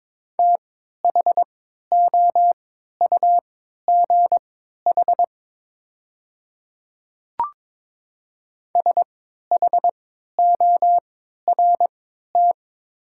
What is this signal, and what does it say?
Morse code, heard as a keyed tone, about 22 wpm, spaced as for 13 wpm, 700 Hz. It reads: THOUGH SHORT
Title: 3x Character Spacing - Top 500 Words - 22wpm